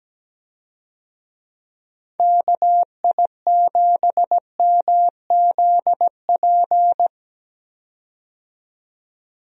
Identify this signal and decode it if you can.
Morse code, heard as a keyed tone, about 17 wpm, 700 Hz. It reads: KI7MZP
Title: USA Call Signs - 6 Characters - 17wpm